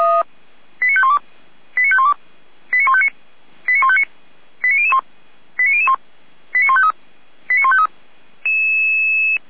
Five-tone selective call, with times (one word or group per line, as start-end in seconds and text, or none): 0.8-1.2 89512
1.8-2.1 89512
2.7-3.1 89159
3.7-4.0 89159
4.6-5.0 89001
5.6-6.0 89001
6.5-6.9 89153
7.5-7.9 89153